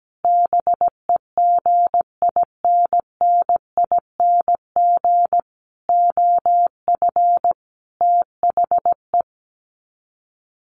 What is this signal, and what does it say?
Morse code, heard as a keyed tone, about 17 wpm, 700 Hz. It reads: BEGINNING OF THE
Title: Sets of 5 Words 17wpm